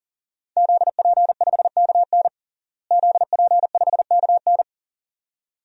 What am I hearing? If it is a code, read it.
Morse code, heard as a keyed tone, about 40 wpm, 700 Hz. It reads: ZP5KN ZP5KN
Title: DXCC-100 Call Signs - 5 Characters - Repeated 2x - 40wpm